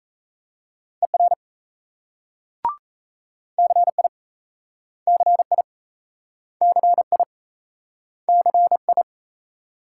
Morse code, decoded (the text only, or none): ER CI CI CI CI